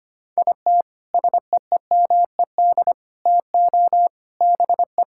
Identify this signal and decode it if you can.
Morse code, heard as a keyed tone, about 25 wpm, 700 Hz. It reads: IT SEEMED TO BE
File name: practice / 25wpm / Sentences with 3-Word Phrases - 25wpm